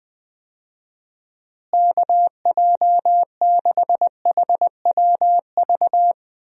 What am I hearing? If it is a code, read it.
Morse code, heard as a keyed tone, about 20 wpm, 700 Hz. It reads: KJ6HWV
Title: USA Call Signs - 6 Characters - Repeated 2x - 20wpm